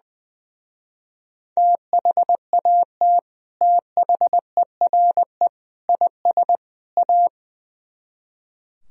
Morse code, heard as a keyed tone, about 20 wpm, 700 Hz. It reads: THAT THERE IS A
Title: Speed-Racing - Sets of 4 Words 20wpm